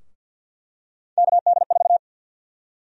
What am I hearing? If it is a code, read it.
Morse code, heard as a keyed tone, about 51 wpm, 700 Hz. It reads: KD4
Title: USA Call Sign Prefixes - Repeated 2x - 50wpm